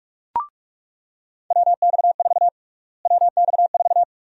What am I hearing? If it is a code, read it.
Morse code, heard as a keyed tone, about 46 wpm, 700 Hz. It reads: WX4 WX4
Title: USA Call Sign Prefixes - Repeated 2x - 45wpm